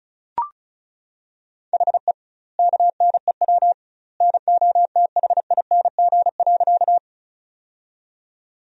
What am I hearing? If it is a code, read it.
Morse code, heard as a keyed tone, about 34 wpm, 700 Hz. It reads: HE KNEW NOTHING.
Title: 2x Word-Spacing - Sentences from Top 500 Words - 35wpm